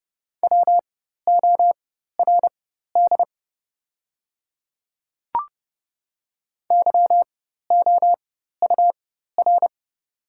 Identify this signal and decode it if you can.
Morse code, heard as a keyed tone, about 30 wpm, spaced as for 14 wpm, 700 Hz. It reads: WORD YOUR
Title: 4x Character Spacing - Top 100 Words - 30wpm